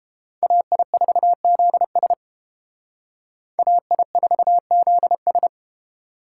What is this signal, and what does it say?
Morse code, heard as a keyed tone, about 29 wpm, 700 Hz. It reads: AI4ZS AI4ZS